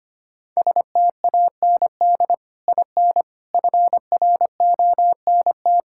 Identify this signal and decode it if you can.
Morse code, heard as a keyed tone, about 25 wpm, 700 Hz. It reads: STAND IN FRONT
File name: practice / 25wpm / Sentences with 3-Word Phrases - 25wpm